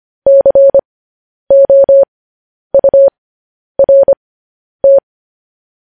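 Morse code, heard as a keyed tone, about 25 wpm, spaced as for 10 wpm, 550 Hz. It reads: COURT
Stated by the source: Morse code transcription of news headlines.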